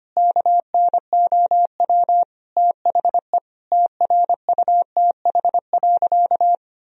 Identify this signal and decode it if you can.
Morse code, heard as a keyed tone, about 25 wpm, 700 Hz. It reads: KNOW THE TRUTH.
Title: Sentences from Top 1000 Words - Review Part 3 - 25wpm